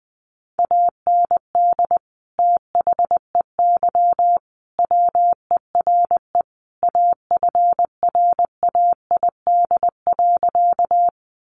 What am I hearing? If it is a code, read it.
Morse code, heard as a keyed tone, about 20 wpm, 700 Hz. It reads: AND THEY WERE AFRAID.
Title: Sentences from Top 1500 Words 20wpm